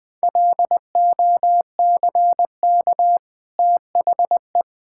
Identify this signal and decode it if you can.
Morse code, heard as a keyed tone, about 21 wpm, 700 Hz. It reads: LOCK THE
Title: Easy Sentences with Hints - 20wpm